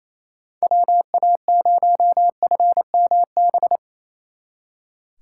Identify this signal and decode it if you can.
Morse code, heard as a keyed tone, about 35 wpm, 700 Hz. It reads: WA0FMB